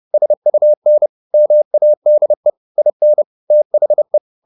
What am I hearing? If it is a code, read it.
Morse code, heard as a keyed tone, about 30 wpm, 600 Hz. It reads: SUN MADE IN THE
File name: the old man and the sea / the old man and the sea 1 30wpm